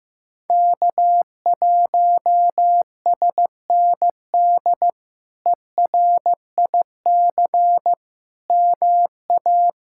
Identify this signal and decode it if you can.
Morse code, heard as a keyed tone, about 16 wpm, 700 Hz. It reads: K1SND ERIC MA
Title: K1USN SST QSO Practice - 15wpm